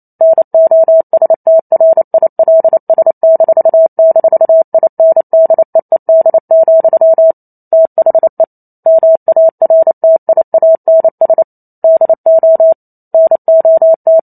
Morse code, heard as a keyed tone, about 28 wpm, 650 Hz. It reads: NOSTRILS--INDEED, THE MARTIANS DO NOT